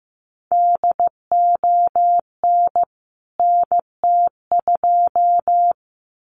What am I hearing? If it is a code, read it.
Morse code, heard as a keyed tone, about 15 wpm, 700 Hz. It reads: DON NT2
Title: CWT Exchange 15wpm